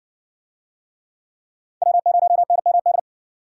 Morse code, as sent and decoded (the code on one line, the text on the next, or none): .-- ----- -. -- -..
W0NMD